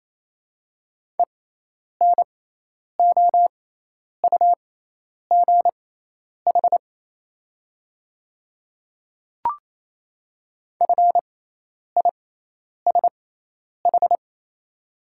Morse code, heard as a keyed tone, about 27 wpm, spaced as for 10 wpm, 690 Hz. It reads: ENOUGH FISH
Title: 6x Character Spacing - Top 500 Words - 28wpm